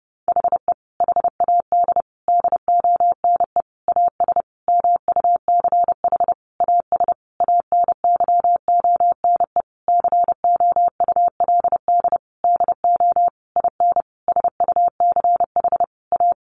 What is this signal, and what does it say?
Morse code, heard as a keyed tone, about 30 wpm, 700 Hz. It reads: HE HAD DONE AS MUCH AS ANYONE COULD DO IN SUCH A
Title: Sentences from Top 1200 Words 30wpm